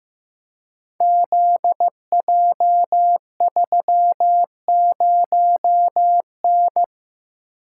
Morse code, heard as a keyed tone, about 15 wpm, 700 Hz. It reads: ZJ30N
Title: Code Groups - 5 Letter:Numbers - 15wpm